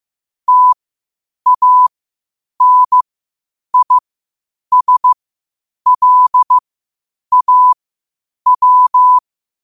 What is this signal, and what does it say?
Morse code, heard as a keyed tone, about 15 wpm, spaced as for 8 wpm, 1000 Hz. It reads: TANISLAW
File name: qotdfiles / 8wpm